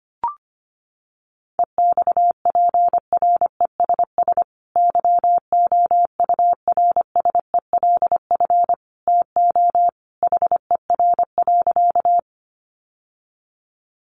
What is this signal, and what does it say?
Morse code, heard as a keyed tone, about 25 wpm, 700 Hz. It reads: EXPRESS YOURSELF TO HER.
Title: Sentences from Top 1100 Words 25wpm